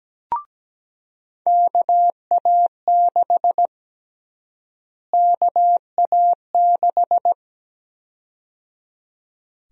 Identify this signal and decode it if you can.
Morse code, heard as a keyed tone, about 17 wpm, 700 Hz. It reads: KA6 KA6
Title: USA Call Sign Prefixes - Repeated 2x - 17wpm